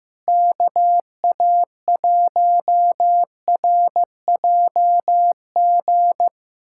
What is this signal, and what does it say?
Morse code, heard as a keyed tone, about 15 wpm, 700 Hz. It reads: KA1RJG
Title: USA Callsign DE Callsign 15wpm